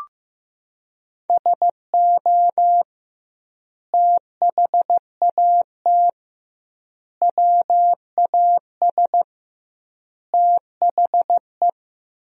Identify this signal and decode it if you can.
Morse code, heard as a keyed tone, about 15 wpm, 700 Hz. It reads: SO THAT WAS THE
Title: 2x Word-Spacing - Sentences from Top 100 Words - 15wpm